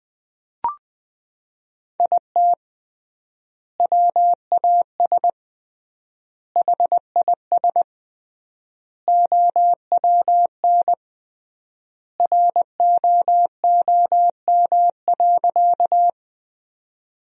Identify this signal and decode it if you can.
Morse code, heard as a keyed tone, about 20 wpm, 700 Hz. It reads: IT WAS HIS OWN ROOM.
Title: 3x Word-Spacing - Sentences from Top 300 Words - 20wpm